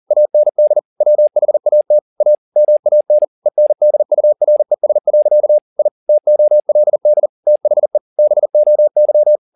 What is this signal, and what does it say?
Morse code, heard as a keyed tone, about 40 wpm, 600 Hz. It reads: AND WHAT A MAN ENDURES. I TOLD THE BOY